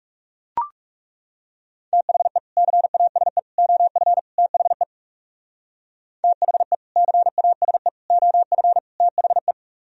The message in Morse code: THE CASE OF THE THE CASE OF THE

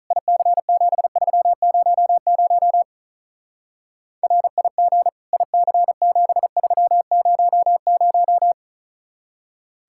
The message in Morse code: IC7300 RIG IC7300

